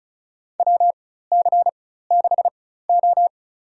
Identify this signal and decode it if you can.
Morse code, heard as a keyed tone, about 34 wpm, spaced as for 16 wpm, 700 Hz. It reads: WC6O
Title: USA Call Signs - 4 Characters - 4x Character Spacing - 35wpm